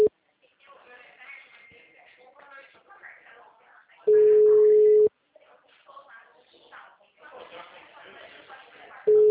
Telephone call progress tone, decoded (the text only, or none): ringback tone